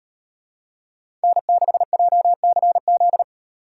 Morse code, heard as a keyed tone, about 38 wpm, 700 Hz. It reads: N6JCZ